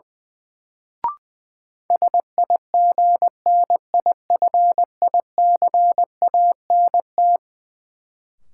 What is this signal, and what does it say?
Morse code, heard as a keyed tone, about 20 wpm, 700 Hz. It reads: SIGNIFICANT
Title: Sentences from Top 1000 Words - Review Part 6 - New Word Drill - 20wpm